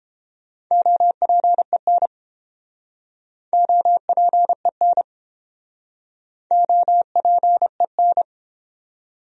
Morse code, heard as a keyed tone, about 29 wpm, 700 Hz. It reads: OPEN OPEN OPEN